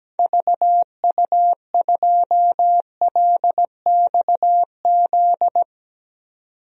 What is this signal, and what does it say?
Morse code, heard as a keyed tone, about 17 wpm, 700 Hz. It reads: VU2LXZ
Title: DXCC-100 Call Signs - 6 Characters - 17wpm